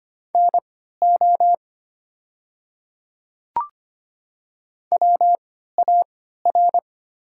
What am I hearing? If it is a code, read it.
Morse code, heard as a keyed tone, about 25 wpm, spaced as for 14 wpm, 700 Hz. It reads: NO WAR
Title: 3x Character Spacing - Top 500 Words - 25wpm